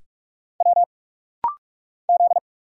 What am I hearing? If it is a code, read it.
Morse code, heard as a keyed tone, about 44 wpm, 700 Hz. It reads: W Z
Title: Single Letter-Number Alt - Rapid-Fire 45wpm